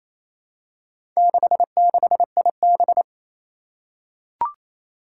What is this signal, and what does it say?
Morse code, heard as a keyed tone, about 27 wpm, 700 Hz. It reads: 66IB
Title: Four Challenging Letter-Number Combinations 28wpm